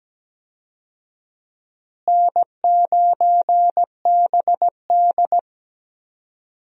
Morse code, heard as a keyed tone, about 17 wpm, 700 Hz. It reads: N9BD
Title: USA Call Signs - 4 Characters - Repeated 2x - 17wpm